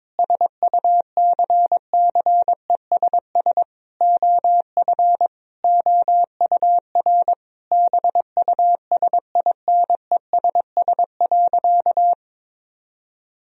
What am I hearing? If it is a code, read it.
Morse code, heard as a keyed tone, about 21 wpm, 700 Hz. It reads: SUCCESS OF OUR BUSINESS.
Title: Sentences from Top 1000 Words - Review Part 4 - 22wpm